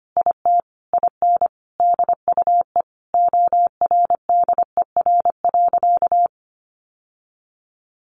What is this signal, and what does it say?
Morse code, heard as a keyed tone, about 25 wpm, 700 Hz. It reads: IT IN DUE ORDER.
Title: Sentences from Top 900 Words 25wpm